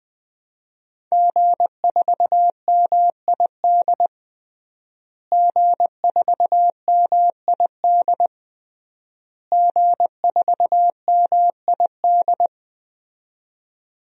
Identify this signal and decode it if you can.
Morse code, heard as a keyed tone, about 20 wpm, 700 Hz. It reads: G4MID G4MID G4MID
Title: DXCC-100 Call Signs - 5 Characters - Repeated 3x - 20wpm